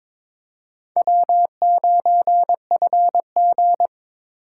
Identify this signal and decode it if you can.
Morse code, heard as a keyed tone, about 26 wpm, 700 Hz. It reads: W9FG
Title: Speed Racing - USA 1x2 Call Signs 22wpm